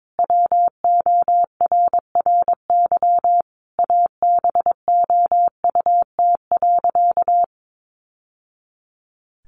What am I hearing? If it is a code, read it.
Morse code, heard as a keyed tone, about 22 wpm, 700 Hz. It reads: WORRY ABOUT.
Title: Sentences from Top 1100 Words 22wpm